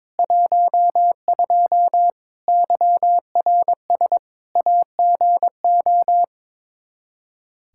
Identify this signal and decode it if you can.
Morse code, heard as a keyed tone, about 22 wpm, 700 Hz. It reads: 12 YRS AGO
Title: K1USN SST QSO Practice 22wpm